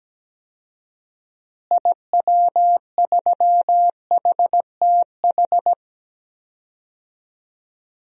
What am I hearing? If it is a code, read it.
Morse code, heard as a keyed tone, about 17 wpm, 700 Hz. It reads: IW3HTH
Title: DXCC-100 Call Signs - 6 Characters - Repeated 3x - 17wpm